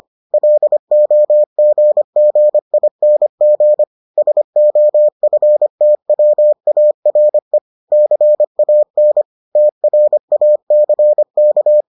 Morse code, heard as a keyed tone, about 25 wpm, 600 Hz. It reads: LOGGING SOFTWARE CAN TRACK